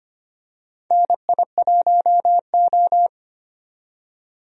Morse code, heard as a keyed tone, about 25 wpm, 700 Hz. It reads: NI1O
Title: USA Call Signs - 4 Characters - 25wpm